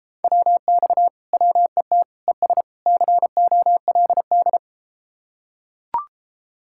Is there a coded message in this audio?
Yes: Morse code from 0.2 to 4.6 s, heard as a keyed tone, about 33 wpm, 700 Hz. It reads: WX WET ES COLD